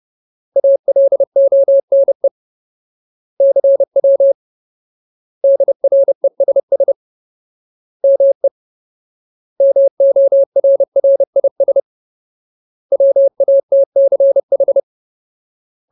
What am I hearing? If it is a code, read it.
Morse code, heard as a keyed tone, about 30 wpm, 550 Hz. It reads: ALONE CW DRESS ME MORRIS WATCH
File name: Ultimate Word List / 30WPM Ultimate